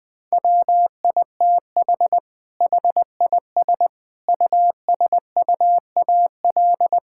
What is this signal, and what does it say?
Morse code, heard as a keyed tone, about 20 wpm, 700 Hz. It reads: WITH HIS USUAL
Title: Sentences from Top 1000 Words - Review Part 6 - 20wpm